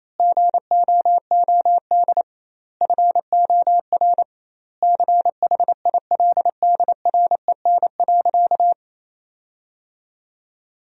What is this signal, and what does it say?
Morse code, heard as a keyed tone, about 28 wpm, 700 Hz. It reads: GOOD FOR CHILDREN.
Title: 2x Word-Spacing - Sentences from Top 400 Words - 28wpm